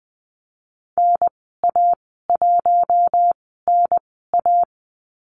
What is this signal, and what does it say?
Morse code, heard as a keyed tone, about 20 wpm, spaced as for 15 wpm, 700 Hz. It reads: NA1NA
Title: 042 Lesson 8g - Number 1 - Call Signs 20wpm